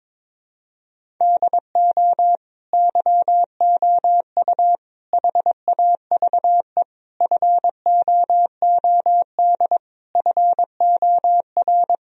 Morse code, heard as a keyed tone, about 22 wpm, 700 Hz. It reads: DO YOU HAVE FOOD FOR
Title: Sentences from Top 1000 Words - Review Part 1 - 22wpm